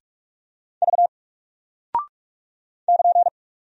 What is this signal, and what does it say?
Morse code, heard as a keyed tone, about 44 wpm, 700 Hz. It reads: V <KN>
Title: Single Character (Letter:Number:Punctuation:Prosign) 45wpm